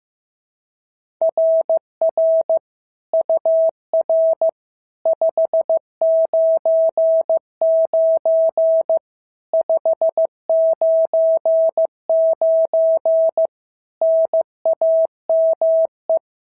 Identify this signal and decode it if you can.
Morse code, heard as a keyed tone, about 15 wpm, 650 Hz. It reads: RR UR 599 599 NAME
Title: SOTA QSO Practice 15wpm